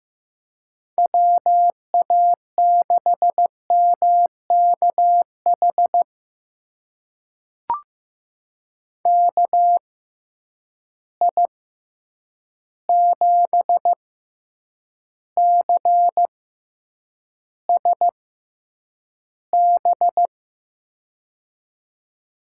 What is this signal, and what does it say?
Morse code, heard as a keyed tone, about 15 wpm, spaced as for 8 wpm, 700 Hz. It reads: WA6MKH KI7CSB
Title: USA Call Signs - 6 Characters - 6x Character Spacing - 15wpm